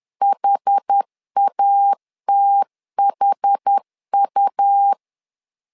Morse code, heard as a keyed tone, about 10 wpm, 800 Hz. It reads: HATHU
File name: LEZIONI / 2 UHDN / 2 UH 10wpm 1714